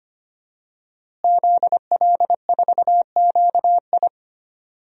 Morse code, heard as a keyed tone, about 25 wpm, 700 Hz. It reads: ZL4QI
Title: DXCC-100 Call Signs - 5 Characters - Repeated 3x - 25wpm